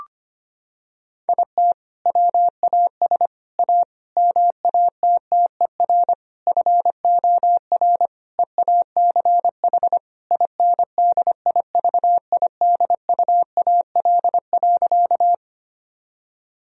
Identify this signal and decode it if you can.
Morse code, heard as a keyed tone, about 25 wpm, 690 Hz. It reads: IT WAS A MATTER FOR EACH INDIVIDUAL.
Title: Sentences from Top 1000 Words - Review Part 4 - 25wpm